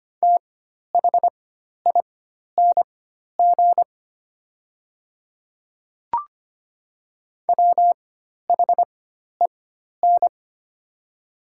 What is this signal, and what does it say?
Morse code, heard as a keyed tone, about 25 wpm, spaced as for 12 wpm, 690 Hz. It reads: THING WHEN
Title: 4x Character Spacing - Top 300 Words - 25wpm